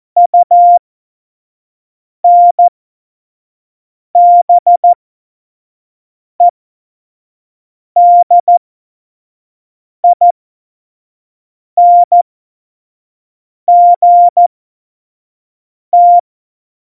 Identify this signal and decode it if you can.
Morse code, heard as a keyed tone, about 14 wpm, spaced as for 5 wpm, 700 Hz. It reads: UNBEDINGT
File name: Morsekurs / Witze / Witz 22. Mai 2018, 25 F70